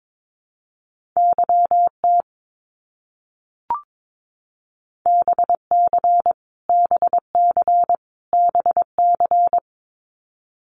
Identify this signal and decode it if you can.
Morse code, heard as a keyed tone, about 22 wpm, 700 Hz. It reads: YT BC BC BC
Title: Canadian Provinces and Territories - Repeated 3x - 22wpm